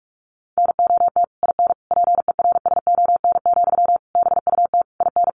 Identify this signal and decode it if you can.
Morse code, heard as a keyed tone, about 44 wpm, 700 Hz. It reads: NOT IN PERSON, BUT IN
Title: Sentences from Top 1400 Words 45wpm